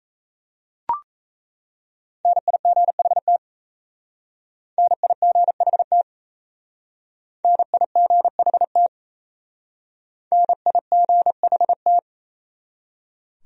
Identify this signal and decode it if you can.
Morse code, heard as a keyed tone, about 36 wpm, 700 Hz. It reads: NIGHT NIGHT NIGHT NIGHT